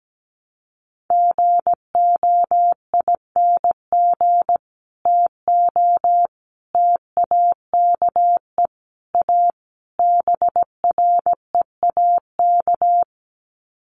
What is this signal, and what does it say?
Morse code, heard as a keyed tone, about 17 wpm, 700 Hz. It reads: GOING TO TAKE A BREAK
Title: Sets of 5 Words 17wpm